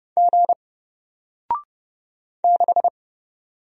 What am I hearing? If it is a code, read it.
Morse code, heard as a keyed tone, about 30 wpm, 700 Hz. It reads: G 6
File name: Single-Letter-Number-30wpm